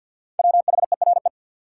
Morse code, heard as a keyed tone, about 52 wpm, 700 Hz. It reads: WHERE
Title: Sentences from Top 200 Words - New Word Drill - 50wpm